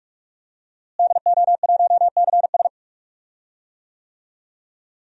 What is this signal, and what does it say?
Morse code, heard as a keyed tone, about 46 wpm, 700 Hz. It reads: DO1CS
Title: DXCC-100 Call Signs - 5 Characters - Repeated 3x - 45wpm